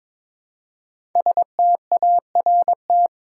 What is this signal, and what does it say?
Morse code, heard as a keyed tone, about 22 wpm, 700 Hz. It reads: START
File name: 5 Letter Words - First 3 Letters Given - 22wpm